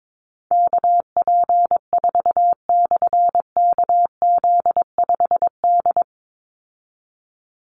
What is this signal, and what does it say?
Morse code, heard as a keyed tone, about 22 wpm, 700 Hz. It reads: KP4/KZ5D
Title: Call Signs Repeated 2x - 22wpm